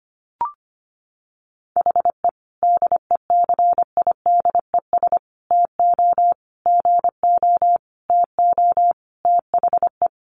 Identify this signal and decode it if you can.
Morse code, heard as a keyed tone, about 25 wpm, 700 Hz. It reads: HE DECIDES TO GO TO THE
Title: Additional Easy Phrases 25wpm